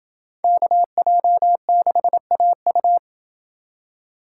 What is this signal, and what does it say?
Morse code, heard as a keyed tone, about 28 wpm, 700 Hz. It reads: KJ6AU